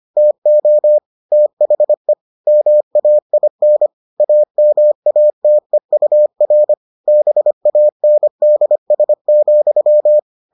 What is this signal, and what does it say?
Morse code, heard as a keyed tone, about 25 wpm, 600 Hz. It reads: TO THE MAIN AMATEUR BANDS,